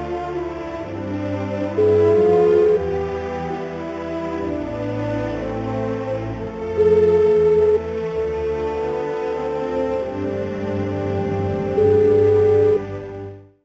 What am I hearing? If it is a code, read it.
Ringback tone (Europe, 425 Hz), with noise.